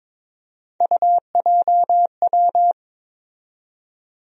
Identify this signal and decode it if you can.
Morse code, heard as a keyed tone, about 22 wpm, 700 Hz. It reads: UJW
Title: USA Call Sign Postfixes - 3 Letters 22wpm